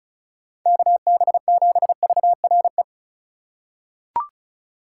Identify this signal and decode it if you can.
Morse code, heard as a keyed tone, about 36 wpm, 700 Hz. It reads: KB7VRE
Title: USA Call Signs - 6 Characters - 2x Character Spacing - 35wpm